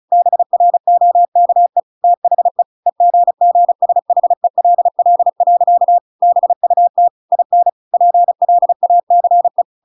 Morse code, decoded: BROKE THE EGGSHELL. BUT IN PLACE

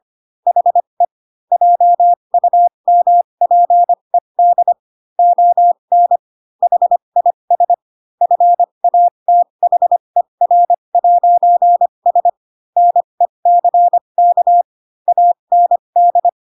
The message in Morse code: HE JUMPED ON HIS FATHER'S NECK AND

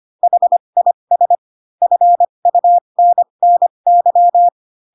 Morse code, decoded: HIS FUNNY

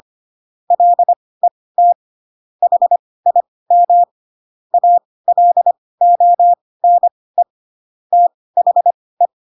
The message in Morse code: LET HIM ALONE THE

